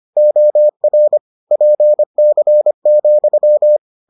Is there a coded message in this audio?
Yes: Morse code from 0.2 to 3.8 s, heard as a keyed tone, about 25 wpm, 600 Hz. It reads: OR PC,